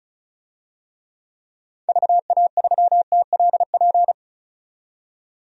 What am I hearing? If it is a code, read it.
Morse code, heard as a keyed tone, about 36 wpm, 700 Hz. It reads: VA3TLP
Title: DXCC-100 Call Signs - 6 Characters - 35wpm